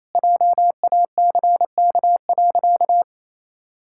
Morse code, heard as a keyed tone, about 28 wpm, 700 Hz. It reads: JACK.
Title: Pangram Challenge v2 - 28wpm